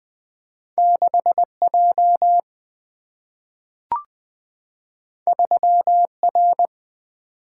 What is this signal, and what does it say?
Morse code, heard as a keyed tone, about 20 wpm, 700 Hz. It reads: 6 J 3 R
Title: Single-Letter Single-Number 20wpm